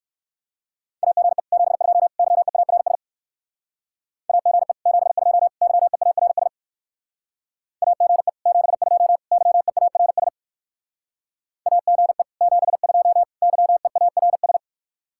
Morse code, heard as a keyed tone, about 62 wpm, 700 Hz. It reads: AGE 72 YEARS AGE 72 YEARS AGE 72 YEARS AGE 72 YEARS